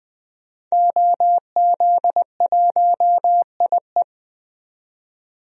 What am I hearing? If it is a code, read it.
Morse code, heard as a keyed tone, about 20 wpm, 700 Hz. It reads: OZ1IE
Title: DXCC-100 Call Signs - 5 Characters - 20wpm